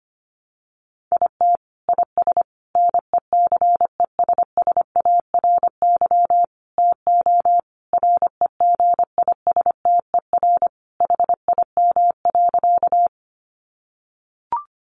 Morse code, heard as a keyed tone, about 25 wpm, 700 Hz. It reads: IT IS NECESSARY TO REGISTER HIM.
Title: Sentences from Top 1200 Words 25wpm